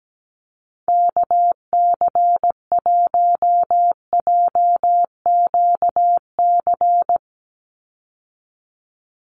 Morse code, heard as a keyed tone, about 17 wpm, 700 Hz. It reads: KC1JQC
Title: USA Call Signs 17wpm